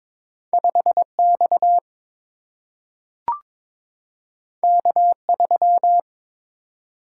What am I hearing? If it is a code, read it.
Morse code, heard as a keyed tone, about 21 wpm, 700 Hz. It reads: 5 X K 3
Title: Single-Letter Single-Number 22wpm